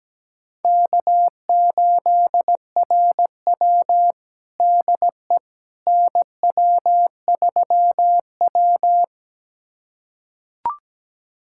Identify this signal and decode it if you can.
Morse code, heard as a keyed tone, about 17 wpm, 700 Hz. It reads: K8RW DE NW3W
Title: USA Callsign DE Callsign - 4 Characters - 17wpm